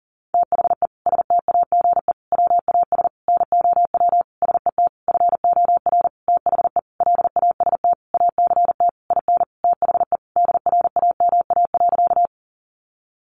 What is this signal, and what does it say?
Morse code, heard as a keyed tone, about 40 wpm, 700 Hz. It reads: THE STAGE WAS NOW SET FOR THE LAST ACT IN THE DRAMA.
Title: Sentences from Top 1500 Words 40wpm